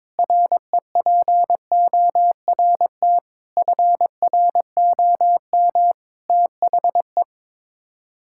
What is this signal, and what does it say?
Morse code, heard as a keyed tone, about 21 wpm, 700 Hz. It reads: REPORT FROM THE
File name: practice / 22wpm / Sets of 7 Words - 22wpm